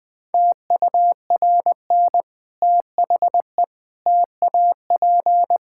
Morse code, heard as a keyed tone, about 20 wpm, 700 Hz. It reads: TURN THE TAP
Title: Difficult Sentences with Hints - 20wpm